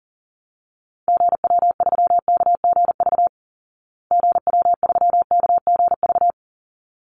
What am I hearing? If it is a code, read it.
Morse code, heard as a keyed tone, about 40 wpm, 700 Hz. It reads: GW3KGV GW3KGV
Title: Call Signs Repeated 3x - 40wpm